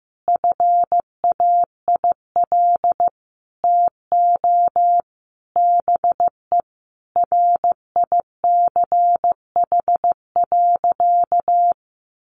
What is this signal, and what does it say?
Morse code, heard as a keyed tone, about 14 wpm, 700 Hz. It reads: FAIL TO BE RICH.
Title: Sentences from Top 1100 Words 15wpm